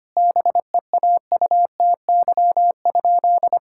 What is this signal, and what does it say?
Morse code, heard as a keyed tone, about 25 wpm, 700 Hz. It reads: BEAUTY?
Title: Sentences from Top 1000 Words - Review Part 5 - 25wpm